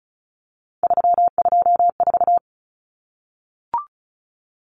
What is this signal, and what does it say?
Morse code, heard as a keyed tone, about 34 wpm, 700 Hz. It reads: 324
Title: Common Cut Numbers - 3-Digit 35wpm